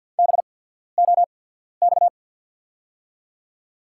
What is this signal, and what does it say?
Morse code, heard as a keyed tone, about 53 wpm, spaced as for 14 wpm, 690 Hz. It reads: BOX
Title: 8x Character Spacing - Top 500 Words - 50wpm